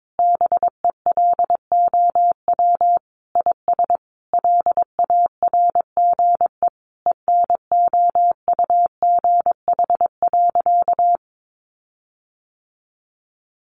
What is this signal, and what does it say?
Morse code, heard as a keyed tone, about 22 wpm, 700 Hz. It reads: BELOW IS LARGE ENOUGH.
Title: Sentences from Top 1500 Words 22wpm